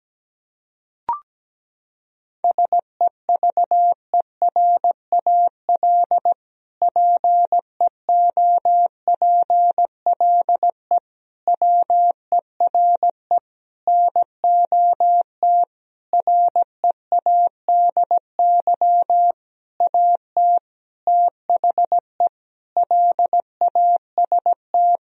Morse code, heard as a keyed tone, about 17 wpm, 700 Hz. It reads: SEVERAL PEOPLE WERE NOT READY AT THE LAST
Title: Sentences from Top 1000 Words - Review Part 2 - 17wpm